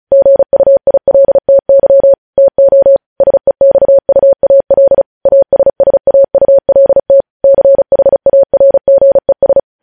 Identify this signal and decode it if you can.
Morse code, heard as a keyed tone, about 36 wpm, 550 Hz. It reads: GUILTY TO SEXUAL ASSAULT CHARGES
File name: podcast / News headlines-1x-35x35